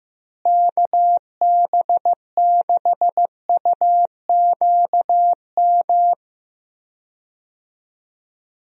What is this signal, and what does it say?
Morse code, heard as a keyed tone, about 15 wpm, 700 Hz. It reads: KB6UQM
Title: USA Call Signs - 6 Characters - Repeated 2x - 15wpm